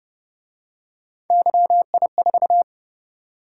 Morse code, heard as a keyed tone, about 27 wpm, 700 Hz. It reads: YI4